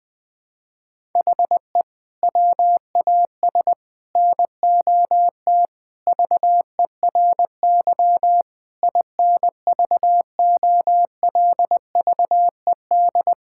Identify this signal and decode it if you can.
Morse code, heard as a keyed tone, about 20 wpm, 700 Hz. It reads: HE WAS NOT VERY INVOLVED
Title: Sentences from Top 1000 Words - Review Part 5 - 20wpm